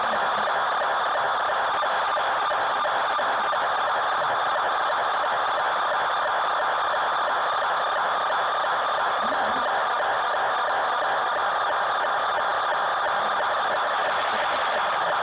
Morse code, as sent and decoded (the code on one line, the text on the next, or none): none